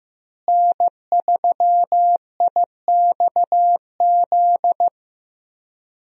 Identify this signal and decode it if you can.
Morse code, heard as a keyed tone, about 15 wpm, 700 Hz. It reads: N3IXZ
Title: USA Call Signs - 5 Characters - Repeated 2x - 15wpm